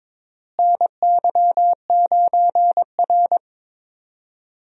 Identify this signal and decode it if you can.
Morse code, heard as a keyed tone, about 22 wpm, 700 Hz. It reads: NY9R
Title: USA Call Signs - 4 Characters - Repeated 3x - 22wpm